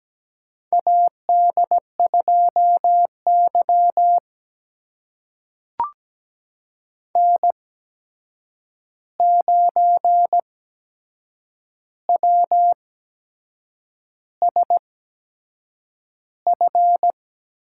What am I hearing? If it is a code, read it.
Morse code, heard as a keyed tone, about 17 wpm, spaced as for 7 wpm, 700 Hz. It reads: AD2Y N9WSF
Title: USA Call Signs - 8x Character Spacing - 17wpm